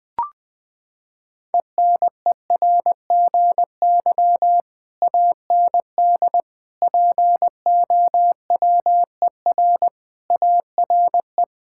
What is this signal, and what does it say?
Morse code, heard as a keyed tone, about 20 wpm, 700 Hz. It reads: ENERGY AND POWER ARE
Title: Sentences from Top 1000 Words - Review Part 6 - 20wpm